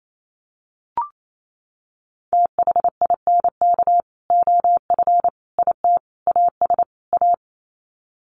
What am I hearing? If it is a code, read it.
Morse code, heard as a keyed tone, about 28 wpm, 700 Hz. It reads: THINK OF IT AS A
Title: Sets of 5 Words Encore2 28wpm